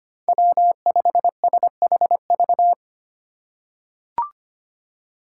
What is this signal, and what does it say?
Morse code, heard as a keyed tone, about 25 wpm, 700 Hz. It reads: W5SHV
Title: USA Call Signs - 5 Characters - 4x Character Spacing - 25wpm